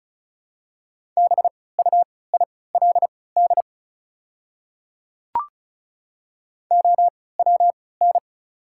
Morse code, heard as a keyed tone, about 34 wpm, spaced as for 20 wpm, 690 Hz. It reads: BUILD OWN
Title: 3x Character Spacing - Top 300 Words - 35wpm